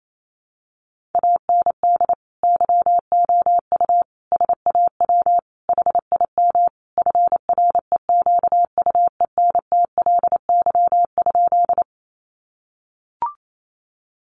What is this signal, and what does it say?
Morse code, heard as a keyed tone, about 28 wpm, 700 Hz. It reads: AND YOU SAW HIM FREQUENTLY?
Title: Sentences from Top 1300 Words 28wpm